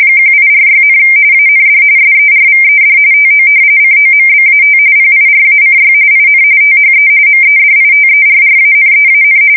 RTTY, about 45.45 baud. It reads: 1846, AND SERVED TO COMBAT PIRACY IN THE MEDITERRANEAN
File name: fsk170 / usnavy1